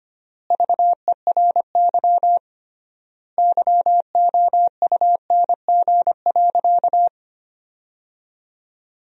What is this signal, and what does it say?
Morse code, heard as a keyed tone, about 25 wpm, 700 Hz. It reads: VERY YOUNG.
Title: 3x Word-Spacing - Sentences from Top 400 Words - 25wpm